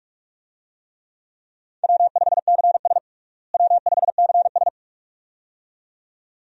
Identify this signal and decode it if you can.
Morse code, heard as a keyed tone, about 46 wpm, 700 Hz. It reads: W5CS W5CS
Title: USA Call Signs - 4 Characters - Repeated 2x - 45wpm